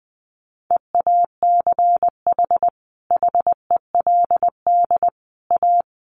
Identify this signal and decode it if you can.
Morse code, heard as a keyed tone, about 20 wpm, 700 Hz. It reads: EACH HELD A
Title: A Sound Of Thunder 20wpm